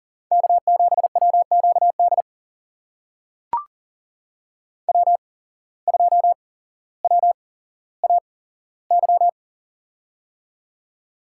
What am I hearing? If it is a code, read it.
Morse code, heard as a keyed tone, about 40 wpm, spaced as for 17 wpm, 700 Hz. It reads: K7WQD W2WAY
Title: USA Call Signs - 8x Character Spacing - 40wpm